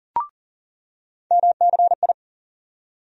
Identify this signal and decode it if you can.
Morse code, heard as a keyed tone, about 40 wpm, 700 Hz. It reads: MCI
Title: USA Call Sign Postfixes - 3 Letters 40wpm